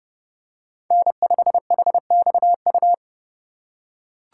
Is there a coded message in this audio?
Yes: Morse code from 0.9 to 2.9 s, heard as a keyed tone, about 29 wpm, 700 Hz. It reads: N5HXU